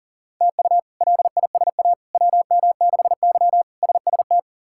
Morse code, heard as a keyed tone, about 40 wpm, 700 Hz. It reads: TU LISA WM6Y SST
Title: K1USN SST QSO Practice - 40wpm